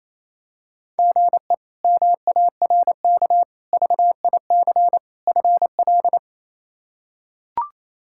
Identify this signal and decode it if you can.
Morse code, heard as a keyed tone, about 27 wpm, 700 Hz. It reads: GE MARK VIC FL
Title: K1USN SST QSO Practice - 28wpm